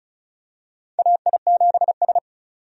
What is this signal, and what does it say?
Morse code, heard as a keyed tone, about 34 wpm, 700 Hz. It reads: AI7S
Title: USA Call Signs - 4 Characters - 35wpm